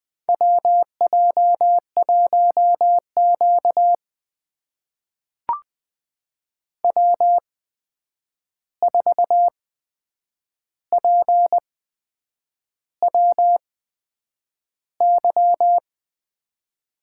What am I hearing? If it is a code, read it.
Morse code, heard as a keyed tone, about 21 wpm, spaced as for 8 wpm, 700 Hz. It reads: WJ1Q W4PWY
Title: USA Call Signs - 8x Character Spacing - 20wpm